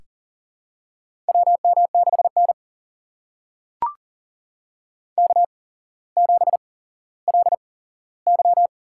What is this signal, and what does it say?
Morse code, heard as a keyed tone, about 42 wpm, spaced as for 17 wpm, 700 Hz. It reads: WM6N K7LY
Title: USA Call Signs - 4 Characters - 8x Character Spacing - 40wpm